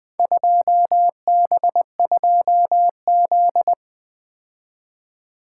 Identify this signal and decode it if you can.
Morse code, heard as a keyed tone, about 20 wpm, 700 Hz. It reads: 2B2Z
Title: Speed-Racing - Four Letter-Number Combinations - 20wpm